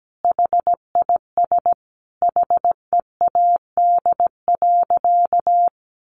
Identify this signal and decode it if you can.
Morse code, heard as a keyed tone, about 17 wpm, 700 Hz. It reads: HIS HEAD.
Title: Sentences from Top 1100 Words 17wpm